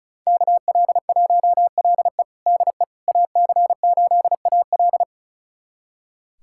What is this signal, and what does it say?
Morse code, heard as a keyed tone, about 36 wpm, 700 Hz. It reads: KL1LE DE AC8AL
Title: USA Callsign DE Callsign - 5 Characters - 35wpm